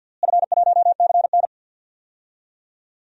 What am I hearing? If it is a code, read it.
Morse code, heard as a keyed tone, about 52 wpm, 700 Hz. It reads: F1CN
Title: DXCC-100 Call Signs - 4 Characters - Repeated 3x - 50wpm